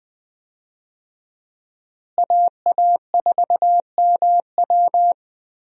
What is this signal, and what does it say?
Morse code, heard as a keyed tone, about 20 wpm, 700 Hz. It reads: AA4MW
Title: USA Call Signs - 5 Characters - Repeated 2x - 20wpm